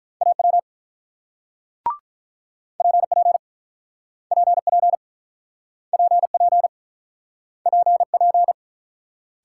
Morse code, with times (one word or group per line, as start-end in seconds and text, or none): 0.2-0.6 AW
2.8-3.4 PP
4.3-5.0 PP
5.9-6.7 PP
7.7-8.5 PP